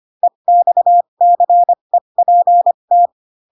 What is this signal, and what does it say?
Morse code, heard as a keyed tone, about 25 wpm, spaced as for 22 wpm, 700 Hz. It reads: EXCEPT
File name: Pinocchio / Pinocchio ch7 18 25 ik1hge